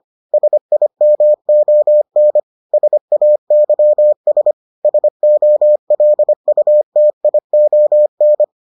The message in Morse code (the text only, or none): SIMON SAYS SOLUTION